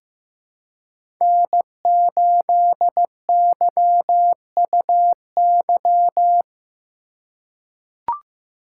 Morse code, heard as a keyed tone, about 15 wpm, 700 Hz. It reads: N8YUY
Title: USA Call Signs - 4x Character Spacing - 15wpm